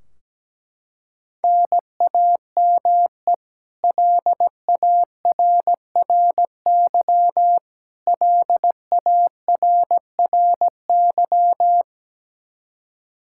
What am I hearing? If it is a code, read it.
Morse code, heard as a keyed tone, about 17 wpm, 700 Hz. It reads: NAME LARRY LARRY
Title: Practice QSO - Exchange 1 - Beginning - 17wpm